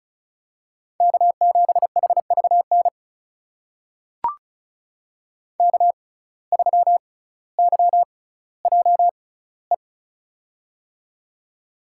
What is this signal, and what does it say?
Morse code, heard as a keyed tone, about 34 wpm, spaced as for 18 wpm, 700 Hz. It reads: K7HVN K3YJE
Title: USA Call Signs - 5 Characters - 6x Character Spacing - 35wpm